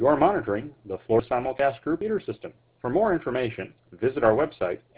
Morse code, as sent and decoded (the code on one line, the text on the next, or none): none